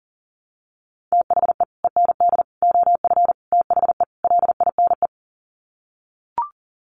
Morse code, heard as a keyed tone, about 40 wpm, 700 Hz. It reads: THE END OF THE LINE
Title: Sets of 5 Words Encore2 40wpm